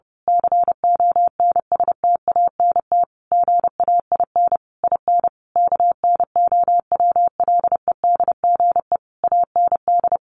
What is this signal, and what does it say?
Morse code, heard as a keyed tone, about 30 wpm, 700 Hz. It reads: CONSTANT GAIN IN KNOWLEDGE AND
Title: Sentences from Top 1500 Words 30wpm